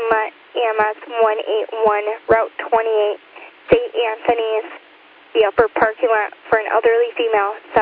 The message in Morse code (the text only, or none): none